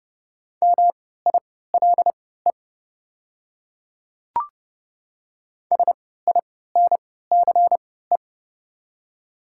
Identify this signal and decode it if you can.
Morse code, heard as a keyed tone, about 30 wpm, spaced as for 17 wpm, 700 Hz. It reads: MILE SINCE
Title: 3x Character Spacing - Top 300 Words - 30wpm